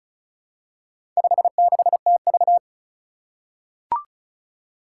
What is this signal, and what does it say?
Morse code, heard as a keyed tone, about 36 wpm, 700 Hz. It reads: 56TV
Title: Four Challenging Letter-Number Combinations 35wpm